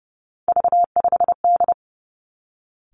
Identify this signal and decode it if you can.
Morse code, heard as a keyed tone, about 30 wpm, 700 Hz. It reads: V5D
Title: Three Challenging Letter-Number Combinations 30wpm